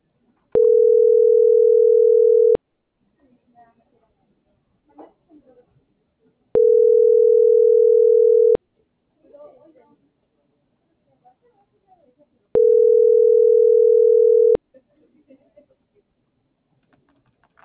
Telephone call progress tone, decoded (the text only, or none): ringback tone